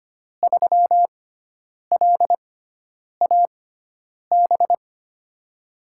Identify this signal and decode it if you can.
Morse code, heard as a keyed tone, about 25 wpm, spaced as for 9 wpm, 700 Hz. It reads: 3LAB
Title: USA Call Signs - 6x Character Spacing - 25wpm